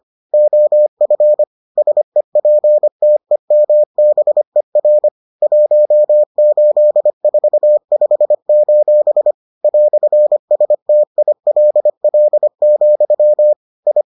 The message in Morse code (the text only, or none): OF SEPTEMBER 1845: "STILL, I